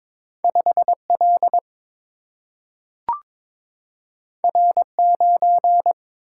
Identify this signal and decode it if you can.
Morse code, heard as a keyed tone, about 22 wpm, 700 Hz. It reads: 5L R9
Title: Single-Letter Single-Number 22wpm